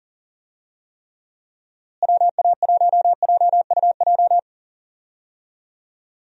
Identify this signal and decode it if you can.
Morse code, heard as a keyed tone, about 40 wpm, 700 Hz. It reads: WA1JUJ
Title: USA Call Signs - 6 Characters - Repeated 3x - 40wpm